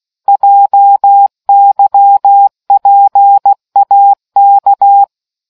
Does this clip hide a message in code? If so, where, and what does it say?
Yes: Morse code from 0.3 to 5.0 s, heard as a keyed tone, about 16 wpm, 800 Hz. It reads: JYPAK